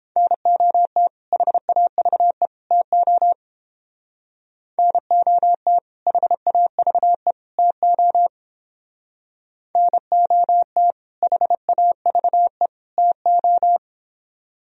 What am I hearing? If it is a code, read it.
Morse code, heard as a keyed tone, about 29 wpm, 700 Hz. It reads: NOT HAVE TO NOT HAVE TO NOT HAVE TO